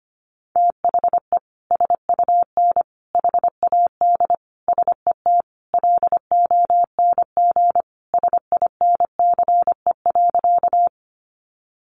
Morse code, heard as a keyed tone, about 25 wpm, 700 Hz. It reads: THE SUN HAD SET LONG SINCE.
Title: Sentences from Top 300 Words 25wpm